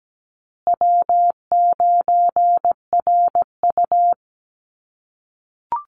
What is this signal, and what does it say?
Morse code, heard as a keyed tone, about 17 wpm, 700 Hz. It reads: W9RU
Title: USA Call Signs Repeated 2x 17wpm